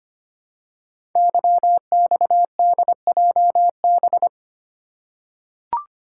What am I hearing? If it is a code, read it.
Morse code, heard as a keyed tone, about 25 wpm, 700 Hz. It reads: YXDJB
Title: Code Groups - 5 Letters - 25wpm